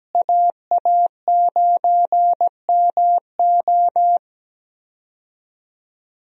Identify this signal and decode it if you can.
Morse code, heard as a keyed tone, about 17 wpm, 700 Hz. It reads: AA9MO
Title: USA Call Signs - 5 Characters - Repeated 2x - 17wpm